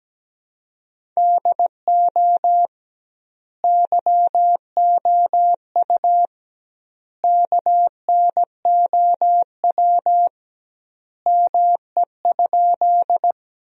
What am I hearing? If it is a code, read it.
Morse code, heard as a keyed tone, about 17 wpm, 700 Hz. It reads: DO YOU KNOW ME?
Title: 2x Word-Spacing - Sentences from Top 200 Words - 17wpm